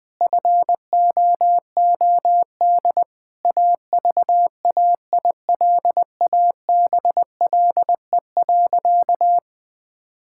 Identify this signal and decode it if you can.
Morse code, heard as a keyed tone, about 21 wpm, 700 Hz. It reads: FOOD AVAILABLE.
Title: Sentences from Top 1000 Words - Review Part 6 - 20wpm